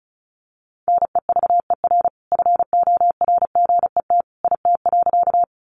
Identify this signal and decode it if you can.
Morse code, heard as a keyed tone, about 34 wpm, 700 Hz. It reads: NEVER FORGET IT.
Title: Sentences from Top 1200 Words 35wpm